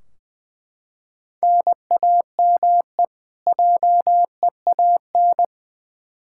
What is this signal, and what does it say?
Morse code, heard as a keyed tone, about 20 wpm, 700 Hz. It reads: NAME JEAN
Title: Practice QSO - Exchange 1 - Beginning - 20wpm